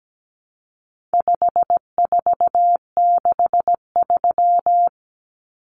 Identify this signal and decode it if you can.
Morse code, heard as a keyed tone, about 17 wpm, 700 Hz. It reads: 5463
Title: Numbers - 4-Digit 17wpm